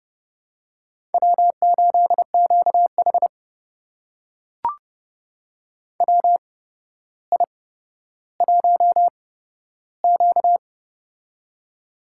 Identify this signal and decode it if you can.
Morse code, heard as a keyed tone, about 30 wpm, spaced as for 13 wpm, 700 Hz. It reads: W8QH WI1Q
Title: USA Call Signs - 4 Characters - 8x Character Spacing - 30wpm